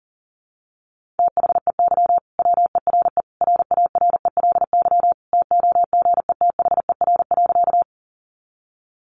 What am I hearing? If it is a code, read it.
Morse code, heard as a keyed tone, about 40 wpm, 700 Hz. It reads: THEY WERE RARELY TOGETHER.
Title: Sentences from Top 1400 Words 40wpm